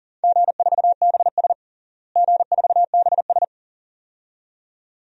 Morse code, heard as a keyed tone, about 40 wpm, 700 Hz. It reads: G4BS G4BS
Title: DXCC-100 Call Signs - 4 Characters - Repeated 2x - 40wpm